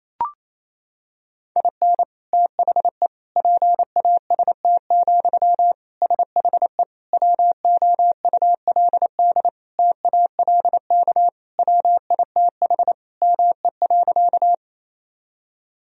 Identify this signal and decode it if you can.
Morse code, heard as a keyed tone, about 28 wpm, 700 Hz. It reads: IN THE PAST, SHE WOULD TALK WITH ME.
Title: Sentences from Top 1000 Words - Review Part 3 - 28wpm